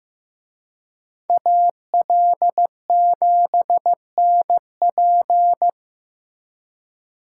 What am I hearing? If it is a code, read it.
Morse code, heard as a keyed tone, about 18 wpm, 700 Hz. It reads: AL7NP
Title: Speed Racing - USA Call Signs 15wpm